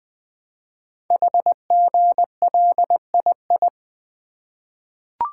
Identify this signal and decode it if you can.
Morse code, heard as a keyed tone, about 20 wpm, 700 Hz. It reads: HGLII
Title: Code Groups - 5 Letters - 20wpm